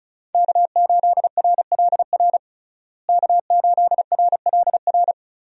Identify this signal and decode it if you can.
Morse code, heard as a keyed tone, about 35 wpm, 700 Hz. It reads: K8RLR K8RLR
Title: USA Call Signs - 5 Characters - Repeated 3x - 35wpm